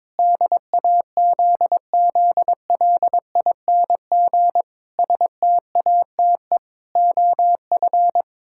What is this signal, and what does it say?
Morse code, heard as a keyed tone, about 22 wpm, 700 Hz. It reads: DAZZLING STATE OF
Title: Pangram Challenge v2 - 22wpm